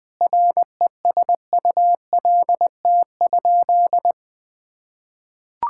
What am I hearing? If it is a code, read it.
Morse code, heard as a keyed tone, about 20 wpm, 700 Hz. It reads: RESULT?
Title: Sentences from Top 1000 Words - Review Part 5 - 20wpm